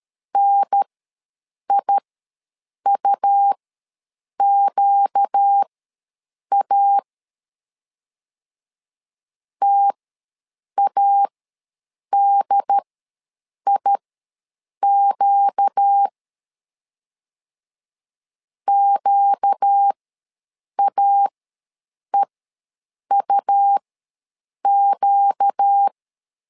Morse code, heard as a keyed tone, about 13 wpm, spaced as for 7 wpm, 800 Hz. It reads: NIUQA TADIQ QAEUQ
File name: LEZIONI / 4 Q ws 12wpm 1912